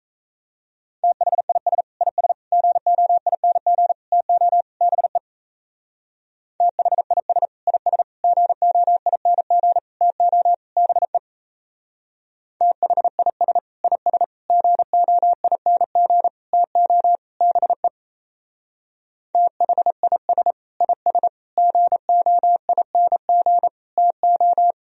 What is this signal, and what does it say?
Morse code, heard as a keyed tone, about 34 wpm, 700 Hz. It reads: THIS IS GOING TO BE THIS IS GOING TO BE THIS IS GOING TO BE THIS IS GOING TO
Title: Speed-Racing - Sets of 5 Words 28wpm